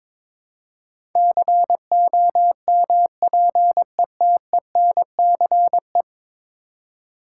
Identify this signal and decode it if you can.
Morse code, heard as a keyed tone, about 22 wpm, 700 Hz. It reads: COMPETENCE
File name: practice / 22wpm / 10 Letter Words - First 3 Letters Given - 22wpm